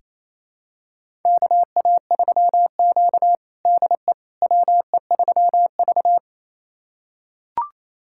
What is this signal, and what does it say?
Morse code, heard as a keyed tone, about 28 wpm, 700 Hz. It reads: KA3Q DE WE3V
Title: USA Callsign DE Callsign - 4 Characters - 28wpm